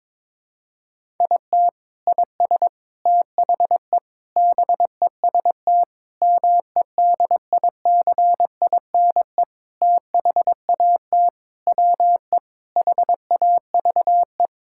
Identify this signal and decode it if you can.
Morse code, heard as a keyed tone, about 22 wpm, 700 Hz. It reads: IT IS THE BEST MEDICINE THAT WE HAVE
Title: Sentences from Top 1000 Words - Review Part 6 - 22wpm